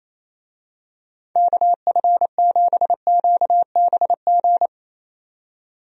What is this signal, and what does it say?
Morse code, heard as a keyed tone, about 27 wpm, 700 Hz. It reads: KF7QBG
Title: USA Call Signs - 6 Characters - 28wpm